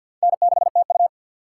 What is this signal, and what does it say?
Morse code, heard as a keyed tone, about 46 wpm, 700 Hz. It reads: N6TU
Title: USA Call Signs - 4 Characters - 2x Character Spacing - 50wpm